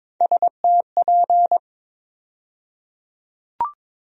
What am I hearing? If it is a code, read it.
Morse code, heard as a keyed tone, about 22 wpm, 700 Hz. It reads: STP
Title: USA Call Sign Postfixes - 3 Letters 22wpm